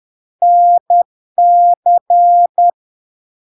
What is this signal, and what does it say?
Morse code, heard as a keyed tone, about 10 wpm, 700 Hz. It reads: NC
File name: LARK-FD-2022-Log-in-CW-10wpm-700-Hz